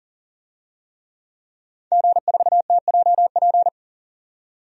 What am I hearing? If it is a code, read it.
Morse code, heard as a keyed tone, about 40 wpm, 700 Hz. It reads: G4TJP
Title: DXCC-100 Call Signs - 5 Characters - 40wpm